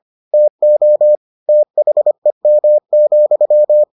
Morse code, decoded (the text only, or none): TO THEM,